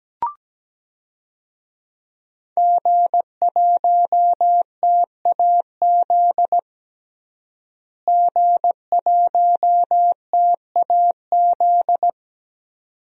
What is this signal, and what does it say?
Morse code, heard as a keyed tone, about 17 wpm, 700 Hz. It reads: G1TAZ G1TAZ
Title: DXCC-100 Call Signs - 5 Characters - Repeated 2x - 17wpm